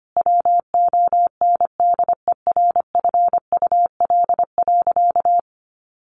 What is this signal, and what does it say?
Morse code, heard as a keyed tone, about 25 wpm, 700 Hz. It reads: WONDERFUL.
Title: Sentences from Top 1100 Words 25wpm